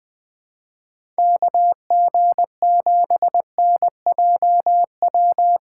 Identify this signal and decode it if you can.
Morse code, heard as a keyed tone, about 20 wpm, 700 Hz. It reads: KG7NJW
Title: USA Call Signs - 6x Character Spacing - 20wpm